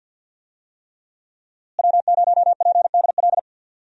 Morse code, heard as a keyed tone, about 47 wpm, 700 Hz. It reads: W0PDL
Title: USA Call Signs - 5 Characters - Repeated 3x - 50wpm